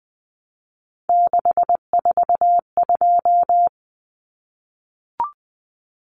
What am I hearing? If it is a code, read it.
Morse code, heard as a keyed tone, about 20 wpm, 700 Hz. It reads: 642
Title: Numbers - 3-Digit 20wpm